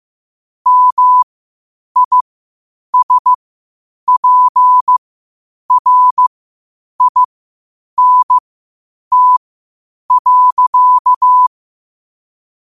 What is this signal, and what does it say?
Morse code, heard as a keyed tone, about 15 wpm, spaced as for 8 wpm, 1000 Hz. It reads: MISPRINT.
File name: qotdfiles / 8wpm